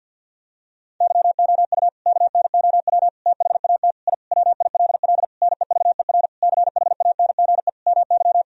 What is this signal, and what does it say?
Morse code, heard as a keyed tone, about 52 wpm, 700 Hz. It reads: YOU KNOW THAT I WILL NEVER CHANGE MY
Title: Sentences from Top 1000 Words - Review Part 1 - 50wpm